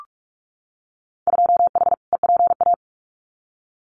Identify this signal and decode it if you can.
Morse code, heard as a keyed tone, about 46 wpm, 700 Hz. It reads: 2H EPA
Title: Winter Field Day Exchanges Encore 45wpm